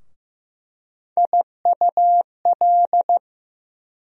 Morse code, heard as a keyed tone, about 15 wpm, 700 Hz. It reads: IUL
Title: USA Call Sign Postfixes - 3 Letters 15wpm